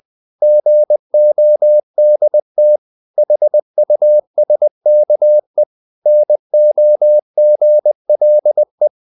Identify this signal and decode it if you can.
Morse code, heard as a keyed tone, about 20 wpm, 600 Hz. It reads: GODT HUSKE NOGLE
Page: Ferie 20 wpm